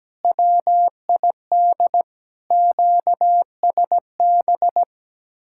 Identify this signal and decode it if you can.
Morse code, heard as a keyed tone, about 17 wpm, 700 Hz. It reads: WID QSB
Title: Speed-Racing Common QSO Phrases - 17wpm